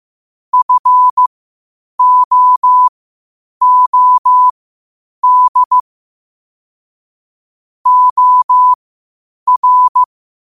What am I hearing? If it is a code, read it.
Morse code, heard as a keyed tone, about 15 wpm, spaced as for 8 wpm, 1000 Hz. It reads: FOOD OR